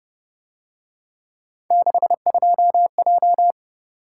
Morse code, heard as a keyed tone, about 29 wpm, 700 Hz. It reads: 62J